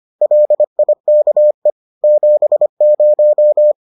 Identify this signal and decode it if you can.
Morse code, heard as a keyed tone, about 25 wpm, 600 Hz. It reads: LIKE 70